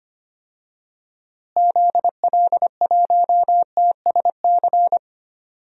Morse code, heard as a keyed tone, about 25 wpm, 700 Hz. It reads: ZL1TSC
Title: DXCC-100 Call Signs - 6 Characters - Repeated 3x - 25wpm